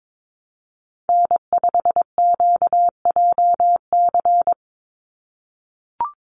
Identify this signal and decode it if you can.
Morse code, heard as a keyed tone, about 22 wpm, 700 Hz. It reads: N5QJC
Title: USA Call Signs 22wpm